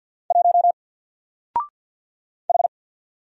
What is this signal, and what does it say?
Morse code, heard as a keyed tone, about 50 wpm, 700 Hz. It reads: 1 H
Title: Single Letter-Number 50wpm